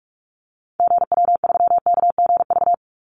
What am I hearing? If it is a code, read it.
Morse code, heard as a keyed tone, about 44 wpm, 700 Hz. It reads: GW3KGV
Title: Call Signs Repeated 3x - 45wpm